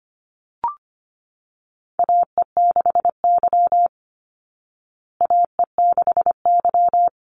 Morse code, heard as a keyed tone, about 25 wpm, 700 Hz. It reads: AE6Y AE6Y
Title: Call Signs Repeated 2x - 25wpm